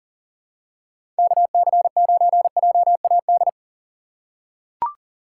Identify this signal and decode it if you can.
Morse code, heard as a keyed tone, about 40 wpm, 700 Hz. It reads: KC9JAD
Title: USA Call Signs - 6 Characters - 2x Character Spacing - 40wpm